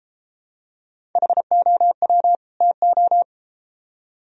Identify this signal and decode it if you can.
Morse code, heard as a keyed tone, about 33 wpm, 700 Hz. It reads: HOW TO